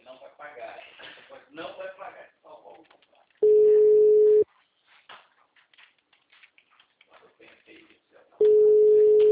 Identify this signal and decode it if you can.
Ringback tone (Europe, 425 Hz), with speech.